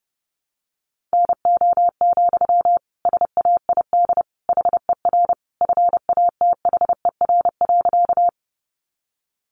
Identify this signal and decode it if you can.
Morse code, heard as a keyed tone, about 30 wpm, 700 Hz. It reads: NO, SAID HER FATHER.
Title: Sentences from Top 300 Words 30wpm